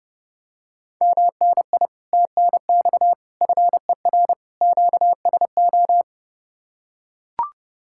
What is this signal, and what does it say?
Morse code, heard as a keyed tone, about 30 wpm, 700 Hz. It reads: MNI TNX FER QSO